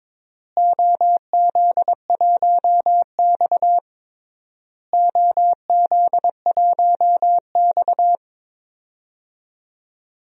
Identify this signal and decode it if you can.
Morse code, heard as a keyed tone, about 22 wpm, 700 Hz. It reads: OZ1X OZ1X
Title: DXCC-100 Call Signs - 4 Characters - Repeated 3x - 22wpm